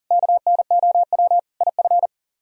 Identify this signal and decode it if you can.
Morse code, heard as a keyed tone, about 52 wpm, 700 Hz. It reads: KNOW IF